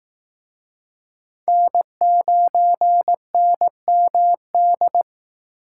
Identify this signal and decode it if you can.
Morse code, heard as a keyed tone, about 18 wpm, 700 Hz. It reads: N9NMD
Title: Speed Racing - USA 1x3 Call Signs 15wpm